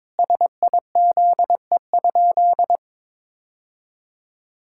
Morse code, heard as a keyed tone, about 22 wpm, 700 Hz. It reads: SIZE?
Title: Sentences from Top 1000 Words - Review Part 6 - 22wpm